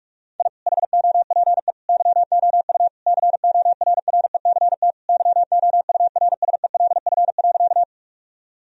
Morse code, heard as a keyed tone, about 42 wpm, 700 Hz. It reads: I HOPE YOU CORRECT YOURSELF.
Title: Sentences from Top 1000 Words - Review Part 2 - 45wpm